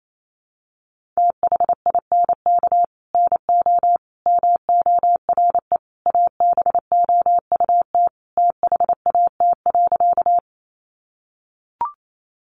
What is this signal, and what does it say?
Morse code, heard as a keyed tone, about 28 wpm, 700 Hz. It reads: THINK NO MORE ABOUT THAT.
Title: Sentences from Top 200 Words 28wpm